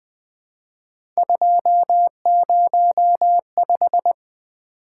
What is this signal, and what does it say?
Morse code, heard as a keyed tone, about 20 wpm, 700 Hz. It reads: 205
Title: Three Letter-Number Combinations - 20wpm